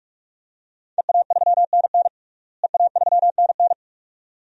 Morse code, heard as a keyed tone, about 46 wpm, 700 Hz. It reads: EA3NN EA3NN
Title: DXCC-100 Call Signs - 5 Characters - Repeated 2x - 45wpm